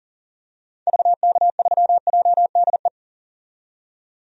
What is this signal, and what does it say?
Morse code, heard as a keyed tone, about 40 wpm, 700 Hz. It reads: VK3JDE
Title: DXCC-100 Call Signs - 6 Characters - 40wpm